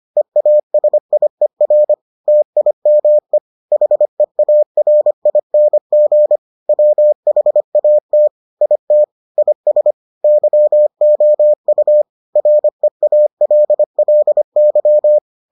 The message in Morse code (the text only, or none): EASIER TIME HEARING WHAT IT IS YOU REALLY